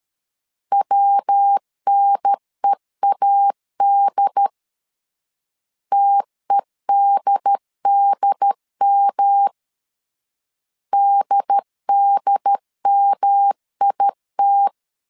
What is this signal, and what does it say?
Morse code, heard as a keyed tone, about 13 wpm, 800 Hz. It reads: WNEAD TEDDM DDMIT
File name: CW / DET / 1 3 1500C S C 12wpm 2750